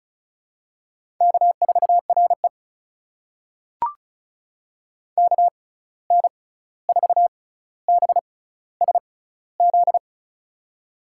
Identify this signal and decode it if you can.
Morse code, heard as a keyed tone, about 36 wpm, spaced as for 16 wpm, 700 Hz. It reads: K4RE KN4BSZ
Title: USA Call Signs - 6x Character Spacing - 35wpm